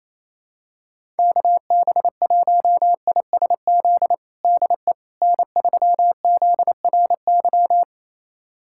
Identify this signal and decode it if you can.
Morse code, heard as a keyed tone, about 28 wpm, 700 Hz. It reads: KB1ISZ DE N3ZRY
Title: USA Callsign DE Callsign 28wpm